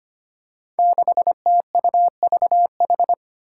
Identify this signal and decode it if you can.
Morse code, heard as a keyed tone, about 25 wpm, 700 Hz. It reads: 6TUVH
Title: Five Challenging Letter-Number Combinations 25wpm